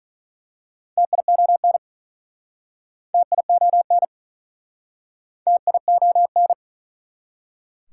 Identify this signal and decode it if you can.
Morse code, heard as a keyed tone, about 46 wpm, 700 Hz. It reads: TION TION TION